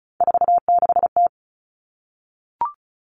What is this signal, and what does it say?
Morse code, heard as a keyed tone, about 34 wpm, 700 Hz. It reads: 46T
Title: All Cut Numbers - 3-Digit 35wpm